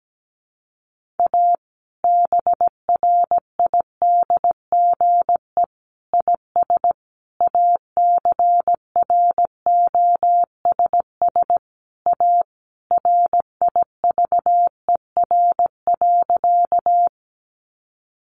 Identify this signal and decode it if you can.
Morse code, heard as a keyed tone, about 17 wpm, 700 Hz. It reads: A BRIDGE IS ACROSS A RIVER.
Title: Sentences from Top 1200 Words 17wpm